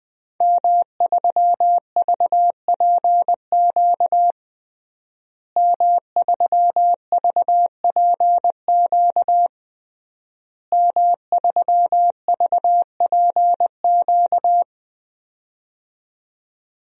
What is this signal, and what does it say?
Morse code, heard as a keyed tone, about 21 wpm, 700 Hz. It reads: M3VPQ M3VPQ M3VPQ
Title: DXCC-100 Call Signs - 5 Characters - Repeated 3x - 20wpm